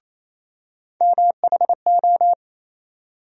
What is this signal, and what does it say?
Morse code, heard as a keyed tone, about 28 wpm, 700 Hz. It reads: MHO
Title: USA Call Sign Postfixes - 3 Letters 28wpm